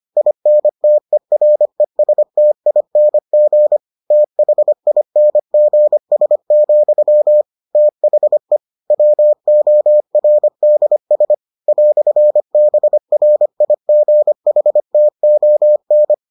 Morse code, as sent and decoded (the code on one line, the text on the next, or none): .. -. - . .-. ... - .. -. --. / - .... .. -. --. ... --..-- / - .... / .-- --- .-. -.. ... / .-..-. -... .-. .. --. .... - --- -.
INTERESTING THINGS, THE WORDS "BRIGHTON